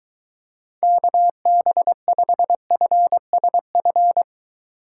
Morse code, heard as a keyed tone, about 23 wpm, 700 Hz. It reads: KB5FSF